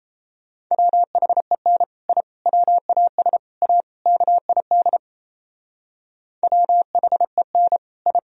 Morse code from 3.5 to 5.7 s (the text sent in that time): A KID